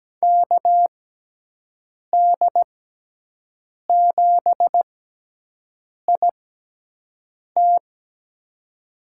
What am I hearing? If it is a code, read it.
Morse code, heard as a keyed tone, about 17 wpm, spaced as for 6 wpm, 700 Hz. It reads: KD7IT
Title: USA Call Signs - 5 Characters - 6x Character Spacing - 17wpm